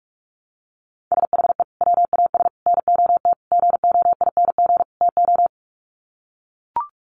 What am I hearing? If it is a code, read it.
Morse code, heard as a keyed tone, about 44 wpm, 700 Hz. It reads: SHE WAS NOT GOING TO
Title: Sets of 5 Words Encore2 45wpm